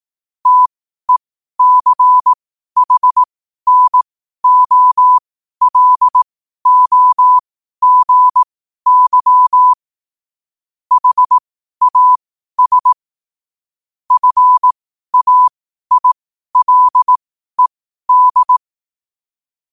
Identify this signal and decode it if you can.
Morse code, heard as a keyed tone, about 18 wpm, spaced as for 12 wpm, 1000 Hz. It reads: TECHNOLOGY HAS FAILED
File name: qotdfiles / 12wpm